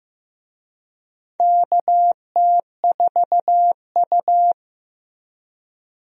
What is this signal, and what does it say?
Morse code, heard as a keyed tone, about 15 wpm, 700 Hz. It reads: KT4U
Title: USA Call Signs - 4 Characters - Repeated 3x - 15wpm